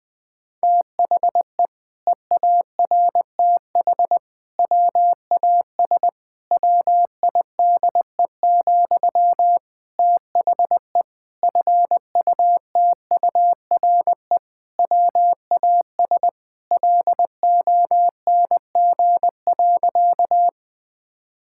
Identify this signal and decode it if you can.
Morse code, heard as a keyed tone, about 20 wpm, 700 Hz. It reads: THE EARTH WAS WIDE, THE FUTURE WAS LONG.
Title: Sentences from Top 1000 Words - Review Part 6 - 20wpm